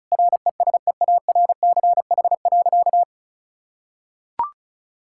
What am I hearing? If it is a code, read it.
Morse code, heard as a keyed tone, about 36 wpm, 700 Hz. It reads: RESEARCH.
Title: Sentences from Top 1000 Words - Review Part 5 - 35wpm